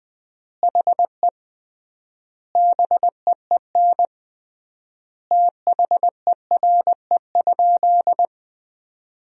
Morse code, heard as a keyed tone, about 20 wpm, 700 Hz. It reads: HE BEEN THERE?
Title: 3x Word-Spacing - Sentences from Top 100 Words - 20wpm